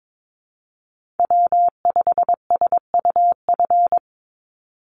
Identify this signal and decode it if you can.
Morse code, heard as a keyed tone, about 22 wpm, 700 Hz. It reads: W5SUF
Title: USA Call Signs 22wpm